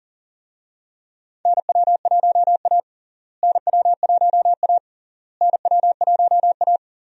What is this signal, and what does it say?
Morse code, heard as a keyed tone, about 40 wpm, 700 Hz. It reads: NW1A NW1A NW1A
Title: USA Call Signs - 4 Characters - Repeated 3x - 40wpm